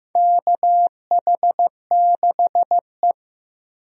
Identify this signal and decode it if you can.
Morse code, heard as a keyed tone, about 15 wpm, 700 Hz. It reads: KH6E
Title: DXCC-100 Call Signs - 4 Characters - Repeated 2x - 15wpm